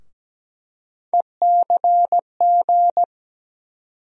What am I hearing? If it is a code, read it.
Morse code, heard as a keyed tone, about 17 wpm, 700 Hz. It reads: ECG
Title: USA Call Sign Postfixes - 3 Letters 17wpm